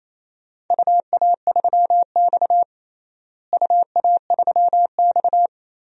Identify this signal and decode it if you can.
Morse code, heard as a keyed tone, about 28 wpm, 700 Hz. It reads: UA3X UA3X
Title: DXCC-100 Call Signs - 4 Characters - Repeated 2x - 28wpm